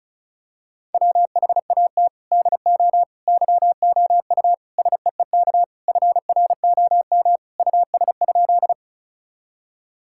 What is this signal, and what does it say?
Morse code, heard as a keyed tone, about 34 wpm, 700 Hz. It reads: WHAT DO YOU SEEK FROM US?
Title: Sentences from Top 1600 Words 35wpm